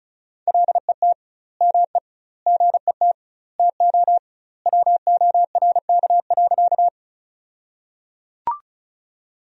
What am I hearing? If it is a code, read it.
Morse code, heard as a keyed tone, about 34 wpm, 700 Hz. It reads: LET ME GET TO WORK.
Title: 2x Word-Spacing - Sentences from Top 300 Words - 35wpm